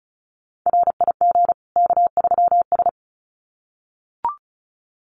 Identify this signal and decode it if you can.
Morse code, heard as a keyed tone, about 34 wpm, 700 Hz. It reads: RIG K3S
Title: Common QSO Phrases 35wpm